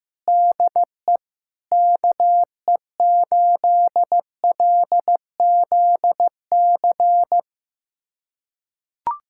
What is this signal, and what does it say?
Morse code, heard as a keyed tone, about 15 wpm, 700 Hz. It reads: DE KE8LZC
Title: USA Callsign DE Callsign - 6 Characters - 15wpm